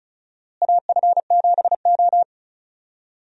Morse code, heard as a keyed tone, about 36 wpm, 700 Hz. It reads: AF7O
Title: USA Call Signs - 4 Characters - Repeated 2x - 35wpm